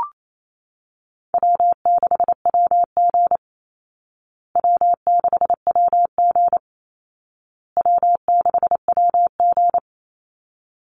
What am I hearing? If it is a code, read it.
Morse code, heard as a keyed tone, about 28 wpm, 700 Hz. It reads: W6WG W6WG W6WG
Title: Call Signs Repeated 2x - 28wpm